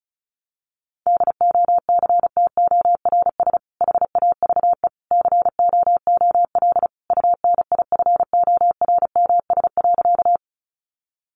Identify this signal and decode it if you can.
Morse code, heard as a keyed tone, about 34 wpm, 700 Hz. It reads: DOCTORS HAVE COOL UNIFORMS.
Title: Easy Phrases 35wpm